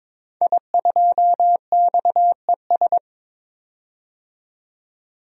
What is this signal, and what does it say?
Morse code, heard as a keyed tone, about 22 wpm, 700 Hz. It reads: I2XES
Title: DXCC-100 Call Signs - 5 Characters - 22wpm